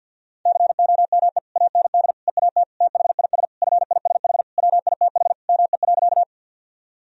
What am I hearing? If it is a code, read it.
Morse code, heard as a keyed tone, about 52 wpm, 700 Hz. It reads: COME AND EAT THIS FISH WITH ME.
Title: Sentences from Top 1000 Words - Review Part 1 - 50wpm